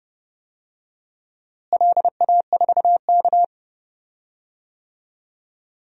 Morse code, heard as a keyed tone, about 30 wpm, 700 Hz. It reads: LA4K
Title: DXCC-100 Call Signs - 4 Characters - Repeated 3x - 30wpm